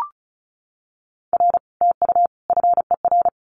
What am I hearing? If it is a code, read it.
Morse code, heard as a keyed tone, about 36 wpm, 700 Hz. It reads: R TU FER
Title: Common QSO Phrases Encore 35wpm